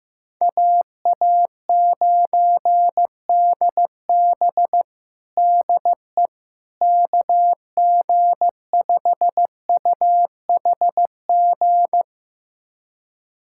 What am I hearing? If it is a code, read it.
Morse code, heard as a keyed tone, about 16 wpm, 700 Hz. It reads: AA9DB DE KG5UHG
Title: USA Callsign DE Callsign 15wpm